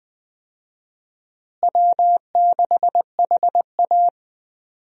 Morse code, heard as a keyed tone, about 20 wpm, 700 Hz. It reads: W6HA
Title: USA Call Signs - 4 Characters - Repeated 3x - 20wpm